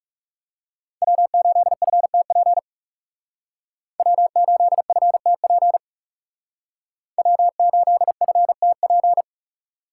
Morse code, heard as a keyed tone, about 38 wpm, 700 Hz. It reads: W8FTP W8FTP W8FTP